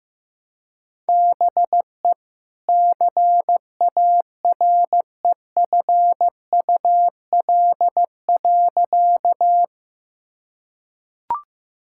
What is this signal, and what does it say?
Morse code, heard as a keyed tone, about 15 wpm, 700 Hz. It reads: BE CAREFUL.
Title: Sentences from Top 1600 Words 15wpm